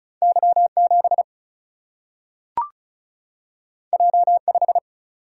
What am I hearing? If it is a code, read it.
Morse code, heard as a keyed tone, about 36 wpm, 700 Hz. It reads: Y 7 J 5
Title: Single-Letter Single-Number 35wpm